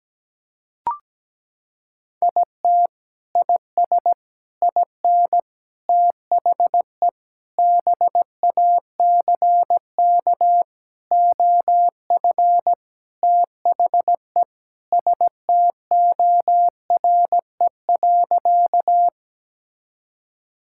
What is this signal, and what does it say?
Morse code, heard as a keyed tone, about 17 wpm, 700 Hz. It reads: IT IS IN THE BACK OF THE STORE.
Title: Sentences from Top 1000 Words - Review Part 5 - 17wpm